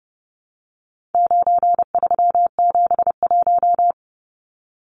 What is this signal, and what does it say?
Morse code, heard as a keyed tone, about 30 wpm, 700 Hz. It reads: 9371
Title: Numbers - 4-Digit 30wpm